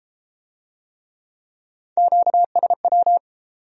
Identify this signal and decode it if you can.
Morse code, heard as a keyed tone, about 33 wpm, 700 Hz. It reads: QSW